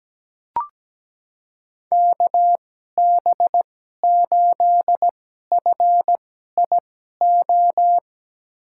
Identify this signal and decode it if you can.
Morse code, heard as a keyed tone, about 17 wpm, spaced as for 12 wpm, 700 Hz. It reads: KB8FIO
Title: USA Call Signs - 2x Character Spacing - 17wpm